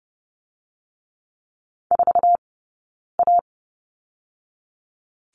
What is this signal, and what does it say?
Morse code, heard as a keyed tone, about 30 wpm, 700 Hz. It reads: 4 A
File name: course / 30:10wpm / 099 Lesson 17d - Letter u - Two Characters 30wpm